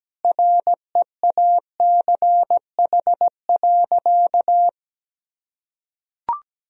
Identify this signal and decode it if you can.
Morse code, heard as a keyed tone, about 17 wpm, 700 Hz. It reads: REACH.
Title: Sentences from Top 1000 Words - Review Part 4 - 17wpm